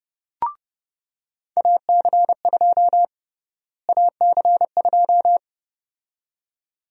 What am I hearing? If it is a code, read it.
Morse code, heard as a keyed tone, about 30 wpm, 700 Hz. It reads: AC2 AC2
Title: USA Call Sign Prefixes - Repeated 2x - 30wpm